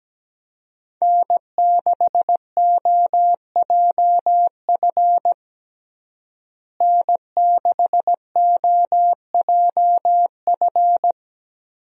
Morse code, heard as a keyed tone, about 17 wpm, 700 Hz. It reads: N6OJF N6OJF
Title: USA Call Signs - 5 Characters - Repeated 2x - 17wpm